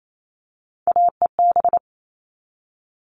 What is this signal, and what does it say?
Morse code, heard as a keyed tone, about 28 wpm, 700 Hz. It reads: AEB
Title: All Cut Numbers - 3-Digit 28wpm